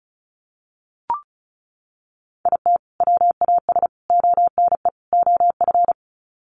Morse code, heard as a keyed tone, about 34 wpm, 700 Hz. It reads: IT WAS ONE OF
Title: Sets of 4 Words 35wpm